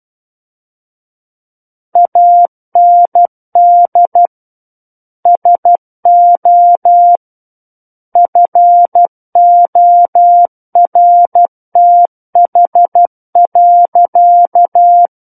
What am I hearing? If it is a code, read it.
Morse code, heard as a keyed tone, about 12 wpm, 700 Hz. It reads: AND SO FORTH.
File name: Book One / 03-Chapter 3 - 12 WPM